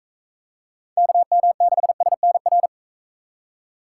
Morse code, heard as a keyed tone, about 41 wpm, 700 Hz. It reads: KM6SNR